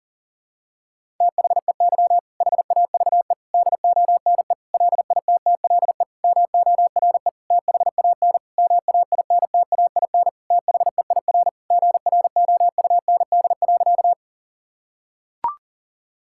Morse code, heard as a keyed tone, about 37 wpm, 700 Hz. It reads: THEY HAVE DONE LITTLE MORE THAN MAINTAIN THEIR GROUND.
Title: Sentences from Top 1000 Words - Review Part 5 - 40wpm